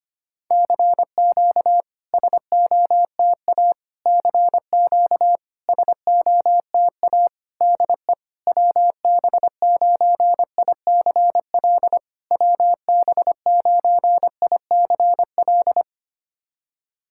Morse code, heard as a keyed tone, about 25 wpm, 700 Hz. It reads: CQ SOTA CQ SOTA DE WB9ICL WB9ICL
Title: SOTA QSO Practice 25wpm